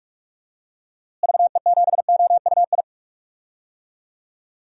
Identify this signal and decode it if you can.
Morse code, heard as a keyed tone, about 43 wpm, 700 Hz. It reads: VE7OUI
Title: DXCC-100 Call Signs - 6 Characters - Repeated 3x - 45wpm